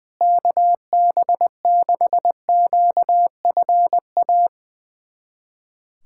Morse code, heard as a keyed tone, about 21 wpm, 700 Hz. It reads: KB6QFA
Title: USA Callsign DE Callsign - 6 Characters - 20wpm